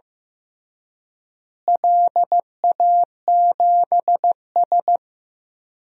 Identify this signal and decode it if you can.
Morse code, heard as a keyed tone, about 15 wpm, 700 Hz. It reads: LA7S
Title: DXCC-100 Call Signs - 4 Characters - Repeated 3x - 15wpm